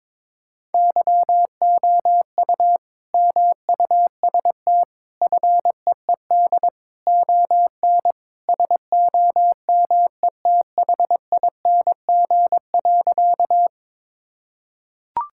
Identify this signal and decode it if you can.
Morse code, heard as a keyed tone, about 22 wpm, 700 Hz. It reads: YOU MUST FEED ON SOMETHING.
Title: Sentences from Top 1600 Words 22wpm